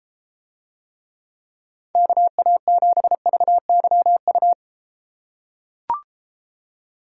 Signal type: Morse code

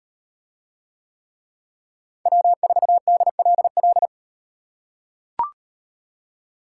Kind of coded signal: Morse code

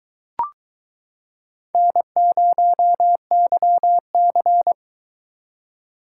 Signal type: Morse code

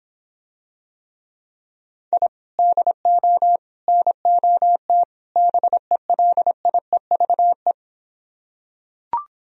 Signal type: Morse code

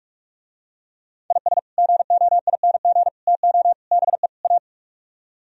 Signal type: Morse code

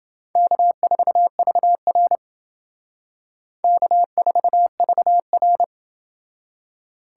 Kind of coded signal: Morse code

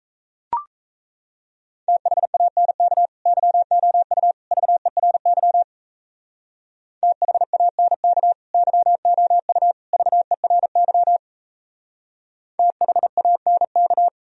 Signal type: Morse code

